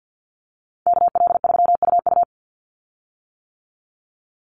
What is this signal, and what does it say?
Morse code, heard as a keyed tone, about 50 wpm, 700 Hz. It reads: KL3UU
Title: USA Call Signs 50wpm